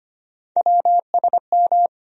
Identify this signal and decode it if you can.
Morse code, heard as a keyed tone, about 25 wpm, 700 Hz. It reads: WSM
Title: USA Call Sign Postfixes - 3 Letters 25wpm